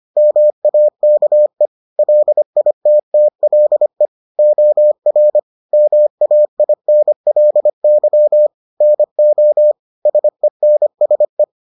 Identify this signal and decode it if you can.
Morse code, heard as a keyed tone, about 25 wpm, 600 Hz. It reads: MAKE LITTLE OR MAINLY NO SENSE